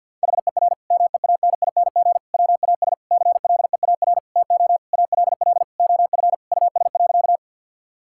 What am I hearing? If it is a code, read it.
Morse code, heard as a keyed tone, about 50 wpm, 700 Hz. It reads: HER MEANING WAS CLEAR TO ALL OF US.
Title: Sentences from Top 1000 Words - Review Part 6 - 50wpm